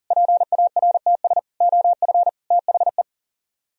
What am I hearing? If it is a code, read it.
Morse code, heard as a keyed tone, about 51 wpm, 700 Hz. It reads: PARTS OF THE